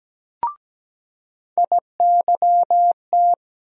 Morse code, heard as a keyed tone, about 17 wpm, 700 Hz. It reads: IYT
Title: USA Call Sign Postfixes - 3 Letters 17wpm